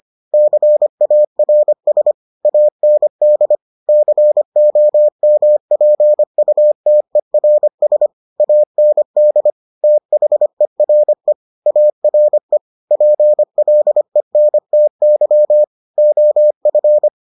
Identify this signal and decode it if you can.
Morse code, heard as a keyed tone, about 25 wpm, 600 Hz. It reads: CARS AND COMPUTERS AND THERE ARE PLENTY OF